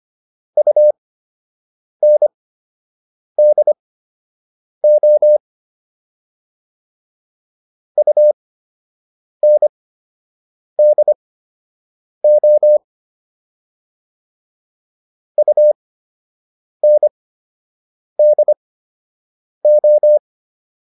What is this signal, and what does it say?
Morse code, heard as a keyed tone, about 25 wpm, spaced as for 7 wpm, 600 Hz. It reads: UNDO UNDO UNDO